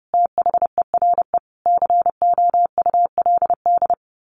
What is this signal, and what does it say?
Morse code, heard as a keyed tone, about 30 wpm, 700 Hz. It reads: THERE COULD
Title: A Sound Of Thunder 30wpm